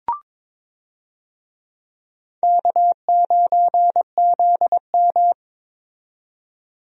Morse code, heard as a keyed tone, about 22 wpm, 700 Hz. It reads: K9ZM
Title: USA Call Signs - 4 Characters - 22wpm